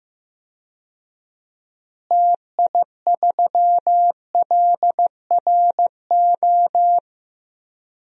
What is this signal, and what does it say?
Morse code, heard as a keyed tone, about 15 wpm, 700 Hz. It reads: TI3LRO
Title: DXCC-100 Call Signs - 6 Characters - Repeated 2x - 15wpm